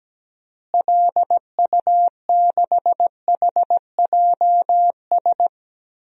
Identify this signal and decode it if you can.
Morse code, heard as a keyed tone, about 17 wpm, 700 Hz. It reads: LU6HJS
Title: DXCC-100 Call Signs - 6 Characters - Repeated 3x - 17wpm